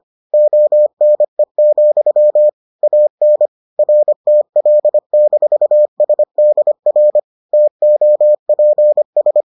Morse code, heard as a keyed tone, about 25 wpm, 600 Hz. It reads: ONE, AN RTL-SDR TOPS